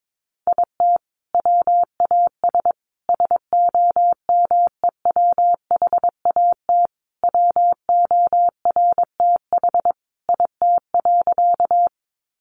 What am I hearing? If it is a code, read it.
Morse code, heard as a keyed tone, about 22 wpm, 700 Hz. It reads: IT WAS SOMEWHAT WORTH IT.
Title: Sentences from Top 1300 Words 22wpm